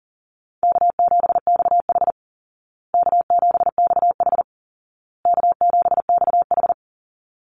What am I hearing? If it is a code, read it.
Morse code, heard as a keyed tone, about 40 wpm, 700 Hz. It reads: K7XH K7XH K7XH
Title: Call Signs Repeated 2x - 40wpm